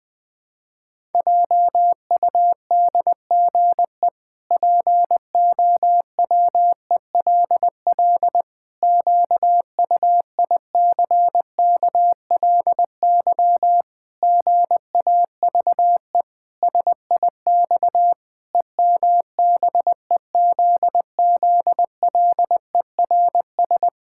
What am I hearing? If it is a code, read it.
Morse code, heard as a keyed tone, about 20 wpm, 700 Hz. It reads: JUDGE POWELL QUICKLY GAVE SIX EMBEZZLERS
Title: Pangram Challenge v2 - 20wpm